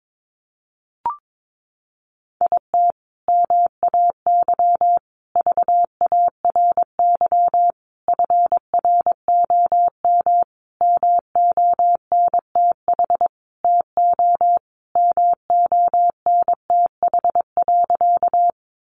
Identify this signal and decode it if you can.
Morse code, heard as a keyed tone, about 22 wpm, 700 Hz. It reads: IT MAY VARY FROM MONTH TO MONTH.
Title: Sentences from Top 1200 Words 22wpm